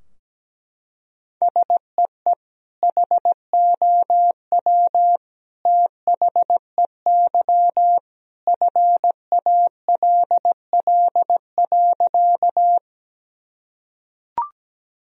Morse code, heard as a keyed tone, about 17 wpm, 700 Hz. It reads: SEE HOW THEY FALL.
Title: Sentences from Top 1000 Words - Review Part 2 - 17wpm